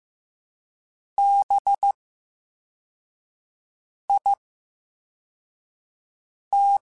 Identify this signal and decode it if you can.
Morse code, heard as a keyed tone, about 15 wpm, 800 Hz. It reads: B I T
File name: Special INGs / Special INGs-mp3s / Slow ing @13